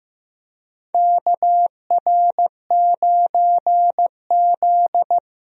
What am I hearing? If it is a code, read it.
Morse code, heard as a keyed tone, about 15 wpm, 700 Hz. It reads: KR9Z
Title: USA Call Signs - 4 Characters - Repeated 3x - 15wpm